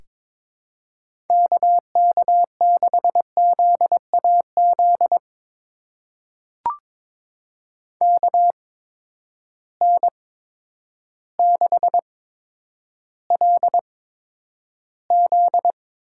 Morse code, heard as a keyed tone, about 21 wpm, spaced as for 10 wpm, 700 Hz. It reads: KK6ZAZ KN6LZ
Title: USA Call Signs - 8x Character Spacing - 22wpm